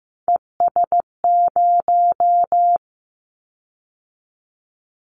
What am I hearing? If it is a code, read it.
Morse code, heard as a keyed tone, about 15 wpm, 700 Hz. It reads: ES0
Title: Callsign Prefixes for Top 100 DXCC - 15wpm